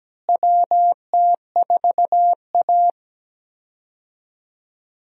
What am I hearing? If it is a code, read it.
Morse code, heard as a keyed tone, about 17 wpm, 700 Hz. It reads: WT4A
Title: USA Call Signs - 4 Characters - Repeated 3x - 17wpm